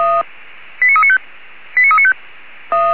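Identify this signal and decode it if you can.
Five-tone selective call (ZVEI), heard as quick runs of five tones, tones of about 80 ms. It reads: 89395 89395
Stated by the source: ZVEI five-tone selective call received on an SDR.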